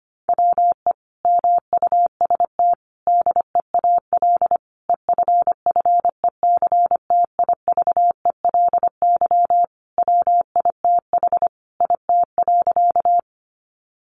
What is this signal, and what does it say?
Morse code, heard as a keyed tone, about 25 wpm, 700 Hz. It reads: WE MUST DEAL EFFECTIVELY WITH IT.
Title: Sentences from Top 1400 Words 25wpm